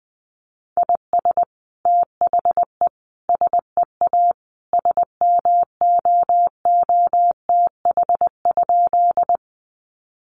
Morse code, heard as a keyed tone, about 20 wpm, 700 Hz. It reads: IS THE SEA SMOOTH?
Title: Sentences from Top 1400 Words 20wpm